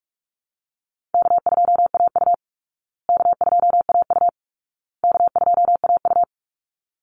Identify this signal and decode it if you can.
Morse code, heard as a keyed tone, about 44 wpm, 700 Hz. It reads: K2AU K2AU K2AU
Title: Easy Call Signs Repeated 3x - 45wpm